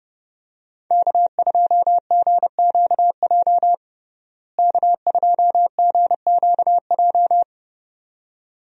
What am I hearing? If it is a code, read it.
Morse code, heard as a keyed tone, about 32 wpm, 700 Hz. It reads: K2GQJ K2GQJ
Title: USA Call Signs - 5 Characters - Repeated 2x - 30wpm